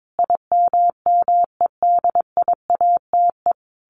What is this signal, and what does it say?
Morse code, heard as a keyed tone, about 22 wpm, 700 Hz. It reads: IMMEDIATE
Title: Sentences from Top 1200 Words 22wpm